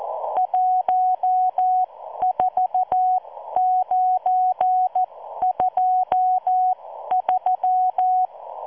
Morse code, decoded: 14923